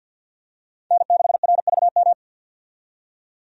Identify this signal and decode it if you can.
Morse code, heard as a keyed tone, about 47 wpm, 700 Hz. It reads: N6RVM
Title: USA Call Signs - 5 Characters - 50wpm